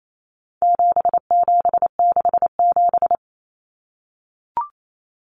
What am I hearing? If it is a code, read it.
Morse code, heard as a keyed tone, about 28 wpm, 700 Hz. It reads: 7767
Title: Numbers - 4-Digit 28wpm